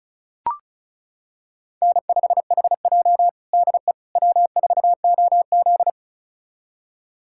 Morse code, heard as a keyed tone, about 36 wpm, 700 Hz. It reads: N5HJ DE W4OZ
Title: USA Callsign DE Callsign - 4 Characters - 35wpm